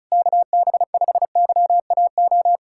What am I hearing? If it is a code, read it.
Morse code, heard as a keyed tone, about 36 wpm, 700 Hz. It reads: KB5YAO
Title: USA Callsign DE Callsign - 6 Characters - 35wpm